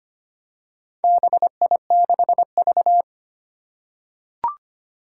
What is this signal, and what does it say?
Morse code, heard as a keyed tone, about 25 wpm, 700 Hz. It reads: BI6V
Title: Four Challenging Letter-Number Combinations 25wpm